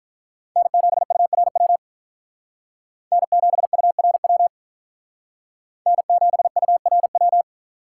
Morse code, heard as a keyed tone, about 46 wpm, 700 Hz. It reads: N7URW N7URW N7URW